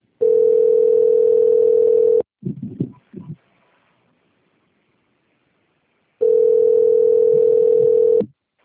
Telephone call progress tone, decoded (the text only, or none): ringback tone